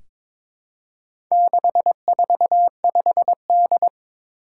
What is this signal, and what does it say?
Morse code, heard as a keyed tone, about 22 wpm, 700 Hz. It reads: 645D
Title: Four Challenging Letter-Number Combinations 22wpm